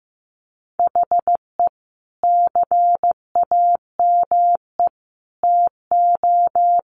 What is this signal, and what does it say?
Morse code, heard as a keyed tone, about 15 wpm, 700 Hz. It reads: HE CAME TO
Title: Sentences from Top 1100 Words 15wpm